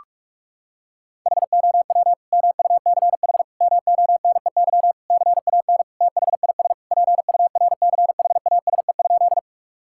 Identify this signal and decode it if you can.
Morse code, heard as a keyed tone, about 41 wpm, 690 Hz. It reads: HOW MUCH MONEY CAN THIS PURCHASE?
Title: Sentences from Top 1000 Words - Review Part 5 - 45wpm